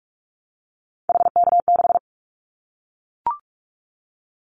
Morse code, heard as a keyed tone, about 44 wpm, 700 Hz. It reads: HK6
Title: Callsign Prefixes for Top 100 DXCC - 45wpm